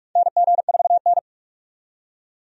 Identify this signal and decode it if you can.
Morse code, heard as a keyed tone, about 46 wpm, 700 Hz. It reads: NG4N
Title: USA Call Signs - 4 Characters - 45wpm